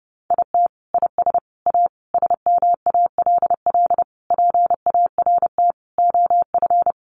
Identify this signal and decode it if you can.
Morse code, heard as a keyed tone, about 30 wpm, 700 Hz. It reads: IT IS A SMALL PART OF
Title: Sentences from Top 1500 Words 30wpm